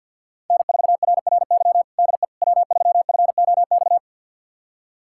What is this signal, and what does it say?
Morse code, heard as a keyed tone, about 49 wpm, 700 Hz. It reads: N4RRY DE W3FOX
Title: USA Callsign DE Callsign - 5 Characters - 50wpm